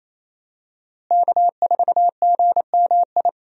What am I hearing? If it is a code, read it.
Morse code, heard as a keyed tone, about 28 wpm, 700 Hz. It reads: K4GMI
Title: USA Call Signs - 5 Characters - Repeated 2x - 28wpm